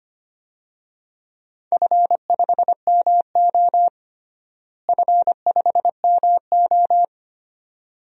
Morse code, heard as a keyed tone, about 25 wpm, 700 Hz. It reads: F5MO F5MO
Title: DXCC-100 Call Signs - 4 Characters - Repeated 2x - 25wpm